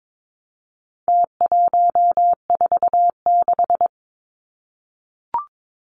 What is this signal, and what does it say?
Morse code, heard as a keyed tone, about 22 wpm, 700 Hz. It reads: T146
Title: Numbers - Contest Style - 4-Digit 22wpm